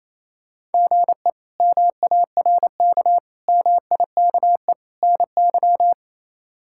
Morse code, heard as a keyed tone, about 27 wpm, 700 Hz. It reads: GE MARK MIKE NY
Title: K1USN SST QSO Practice - 28wpm